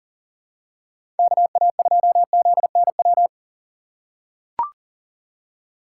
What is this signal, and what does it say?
Morse code, heard as a keyed tone, about 40 wpm, 700 Hz. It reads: KA2ZNW
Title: USA Call Signs - 6 Characters - 2x Character Spacing - 40wpm